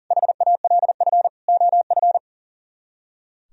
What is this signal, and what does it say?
Morse code, heard as a keyed tone, about 51 wpm, 700 Hz. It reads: HALF OF